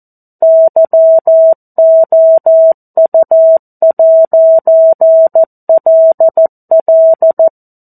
Morse code, heard as a keyed tone, about 14 wpm, 650 Hz. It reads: YOU'LL
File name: 02-Chapter-02 14 WPM